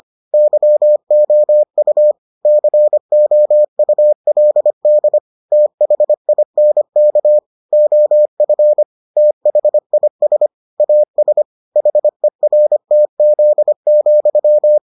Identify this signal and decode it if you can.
Morse code, heard as a keyed tone, about 25 wpm, 600 Hz. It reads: YOU COULD THINK OF THIS AS HERTZ,